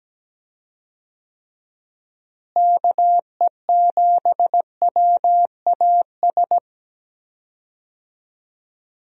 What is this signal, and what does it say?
Morse code, heard as a keyed tone, about 17 wpm, 700 Hz. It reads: KE7WAS
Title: USA Call Signs - 6 Characters - Repeated 3x - 17wpm